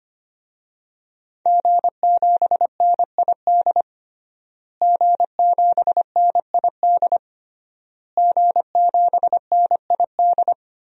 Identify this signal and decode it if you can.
Morse code, heard as a keyed tone, about 25 wpm, 700 Hz. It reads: G7NID G7NID G7NID
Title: DXCC-100 Call Signs - 5 Characters - Repeated 3x - 25wpm